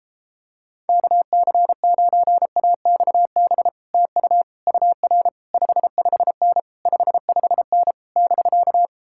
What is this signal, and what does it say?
Morse code, heard as a keyed tone, about 33 wpm, 700 Hz. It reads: KC9AXB TU UR 55N 55N <BK>